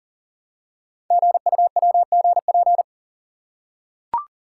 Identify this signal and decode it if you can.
Morse code, heard as a keyed tone, about 43 wpm, 700 Hz. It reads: GUWGP
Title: Code Groups - 5 Letters - 40wpm